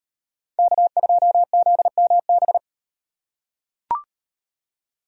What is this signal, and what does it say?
Morse code, heard as a keyed tone, about 35 wpm, 700 Hz. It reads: K2ZMB